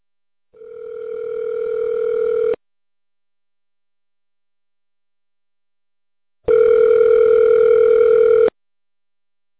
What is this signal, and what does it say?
Ringback tone (North America), 440 + 480 Hz.